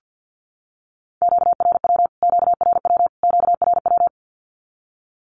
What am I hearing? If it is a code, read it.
Morse code, heard as a keyed tone, about 50 wpm, 700 Hz. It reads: QRW QRW QRW
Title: Q-Codes Repeated 3x 50wpm